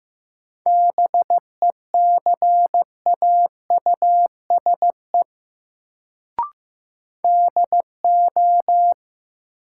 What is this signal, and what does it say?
Morse code, heard as a keyed tone, about 15 wpm, 700 Hz. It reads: BECAUSE DO
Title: Sets of 2 Words - New Word Drill - 15wpm